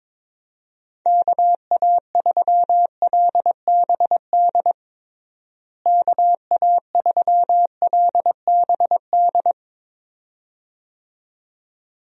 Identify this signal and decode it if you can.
Morse code, heard as a keyed tone, about 22 wpm, 700 Hz. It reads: KA3LBD KA3LBD
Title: USA Call Signs - 6 Characters - Repeated 2x - 22wpm